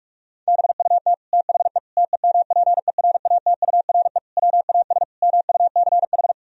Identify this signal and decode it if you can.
Morse code, heard as a keyed tone, about 46 wpm, 700 Hz. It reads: BUT THE TEMPERATURE WAS MUCH
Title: Sentences from Top 1000 Words - Review Part 4 - 45wpm